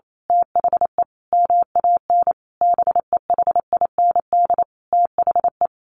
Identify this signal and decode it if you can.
Morse code, heard as a keyed tone, about 28 wpm, 700 Hz. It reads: THE MAN BEHIND THE
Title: A Sound Of Thunder 28wpm